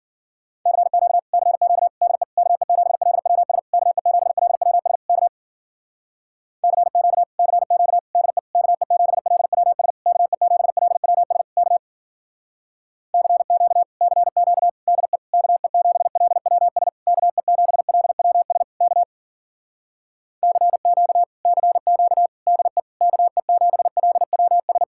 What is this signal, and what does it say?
Morse code, heard as a keyed tone, about 50 wpm, 700 Hz. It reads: CQ CQ DE KE7LWS KE7LWS K CQ CQ DE KE7LWS KE7LWS K CQ CQ DE KE7LWS KE7LWS K CQ CQ DE KE7LWS
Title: Speed-Racing Common QSO Phrases - 40wpm